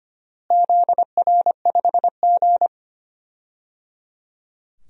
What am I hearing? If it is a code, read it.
Morse code, heard as a keyed tone, about 25 wpm, 700 Hz. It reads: ZR5G
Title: DXCC-100 Call Signs - 4 Characters - Repeated 3x - 25wpm